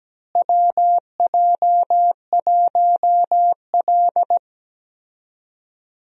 Morse code, heard as a keyed tone, about 17 wpm, 700 Hz. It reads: WJ1L
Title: USA Call Signs - 4 Characters - Repeated 2x - 17wpm